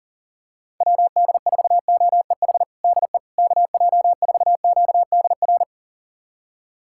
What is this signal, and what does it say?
Morse code, heard as a keyed tone, about 40 wpm, 700 Hz. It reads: WD4OEH DE KJ4QDR
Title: USA Callsign DE Callsign - 6 Characters - 40wpm